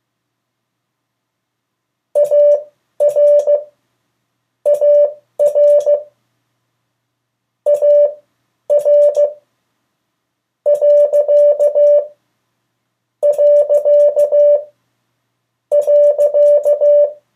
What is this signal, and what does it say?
Morse code, heard as a keyed tone, about 16 wpm, spaced as for 7 wpm, 580 Hz. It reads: ARARAR...
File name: 3of4-words-2